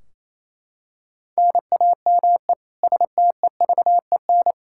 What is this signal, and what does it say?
Morse code, heard as a keyed tone, about 28 wpm, 700 Hz. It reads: NAME STEVEN
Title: Practice QSO - Exchange 1 - Beginning - 28wpm